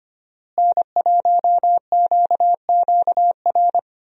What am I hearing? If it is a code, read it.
Morse code, heard as a keyed tone, about 25 wpm, 700 Hz. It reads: N1QQR
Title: USA Call Signs - 5 Characters - Repeated 2x - 25wpm